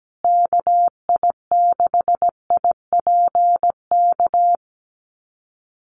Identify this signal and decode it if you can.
Morse code, heard as a keyed tone, about 17 wpm, 700 Hz. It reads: KI6IPK
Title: USA Call Signs Repeated 2x 17wpm